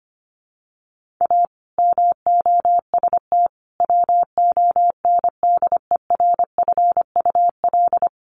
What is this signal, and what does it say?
Morse code, heard as a keyed tone, about 25 wpm, 700 Hz. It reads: A MOST WONDERFUL
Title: Sentences from Top 1100 Words 25wpm